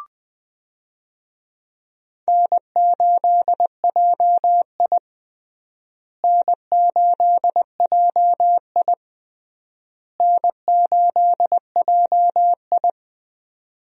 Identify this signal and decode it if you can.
Morse code, heard as a keyed tone, about 20 wpm, 700 Hz. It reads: N8JI N8JI N8JI
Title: USA Call Signs - 4 Characters - Repeated 3x - 20wpm